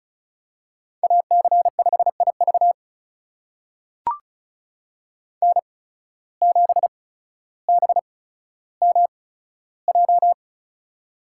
Morse code, heard as a keyed tone, about 36 wpm, spaced as for 15 wpm, 700 Hz. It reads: AC5IV N7BMJ
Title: USA Call Signs - 5 Characters - 8x Character Spacing - 35wpm